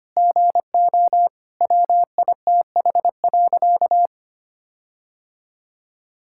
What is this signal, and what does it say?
Morse code, heard as a keyed tone, about 25 wpm, 700 Hz. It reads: GO WITH.
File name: practice / 25wpm / All-Age Appropriate Jokes - 25wpm